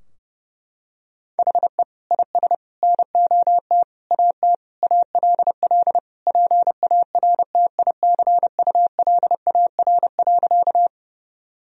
Morse code, heard as a keyed tone, about 30 wpm, 700 Hz. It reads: HE IS NOT AT ALL PARTICULAR.
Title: Sentences from Top 1000 Words - Review Part 4 - 30wpm